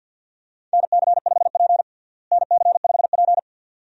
Morse code, heard as a keyed tone, about 46 wpm, 700 Hz. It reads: NC5P NC5P
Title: USA Call Signs - 4 Characters - Repeated 2x - 50wpm